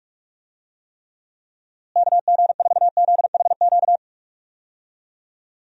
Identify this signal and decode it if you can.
Morse code, heard as a keyed tone, about 46 wpm, 700 Hz. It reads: KG4ZHQ
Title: USA Call Signs - 6 Characters - Repeated 3x - 45wpm